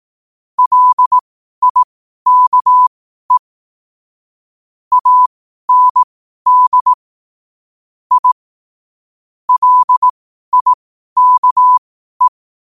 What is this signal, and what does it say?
Morse code, heard as a keyed tone, about 18 wpm, spaced as for 12 wpm, 1000 Hz. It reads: LIKE AND I LIKE
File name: qotdfiles / 12wpm